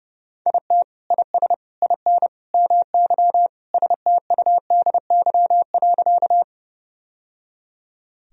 Morse code, heard as a keyed tone, about 32 wpm, 700 Hz. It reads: IT IS IN MY STUDY.
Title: Sentences from Top 1000 Words - Review Part 1 - 30wpm